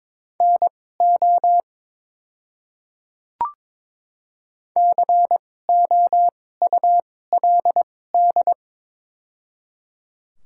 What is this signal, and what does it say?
Morse code, heard as a keyed tone, about 22 wpm, spaced as for 16 wpm, 690 Hz. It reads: NO COULD
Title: 2x Character Spacing - Top 100 Words - 22wpm